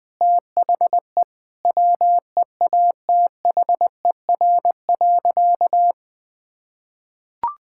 Morse code, heard as a keyed tone, about 21 wpm, 700 Hz. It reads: THE WEATHER.
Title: Sentences with 6-Word Phrases - 20wpm